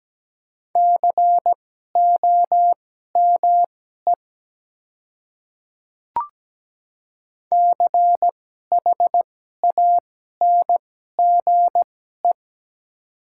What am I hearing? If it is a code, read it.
Morse code, heard as a keyed tone, about 17 wpm, spaced as for 12 wpm, 700 Hz. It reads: COME CHANGE
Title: 2x Character Spacing - Top 200 Words - 17wpm